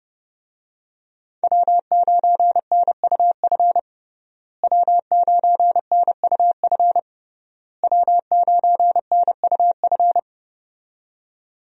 Morse code, heard as a keyed tone, about 32 wpm, 700 Hz. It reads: W9NUF W9NUF W9NUF
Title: USA Call Signs - 5 Characters - Repeated 3x - 30wpm